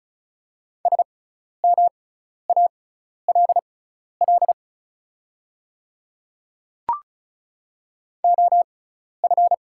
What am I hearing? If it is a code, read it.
Morse code, heard as a keyed tone, about 34 wpm, spaced as for 12 wpm, 700 Hz. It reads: SMALL OF
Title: 6x Character Spacing - Top 200 Words - 35wpm